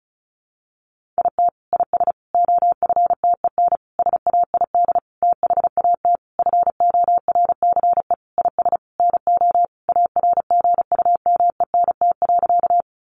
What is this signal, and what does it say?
Morse code, heard as a keyed tone, about 34 wpm, 700 Hz. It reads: IT IS OFTEN SAID THAT FORCE IS NO ARGUMENT.
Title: Sentences from Top 1400 Words 35wpm